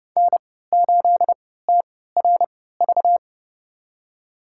Morse code, heard as a keyed tone, about 31 wpm, spaced as for 17 wpm, 700 Hz. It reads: N8TRV
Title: USA Call Signs - 5 Characters - 3x Character Spacing - 30wpm